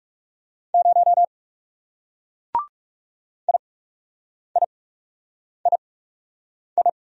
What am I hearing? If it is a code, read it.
Morse code, heard as a keyed tone, about 37 wpm, spaced as for 8 wpm, 700 Hz. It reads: 0 IIII
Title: Speed-Racing Single Letter-Number 30wpm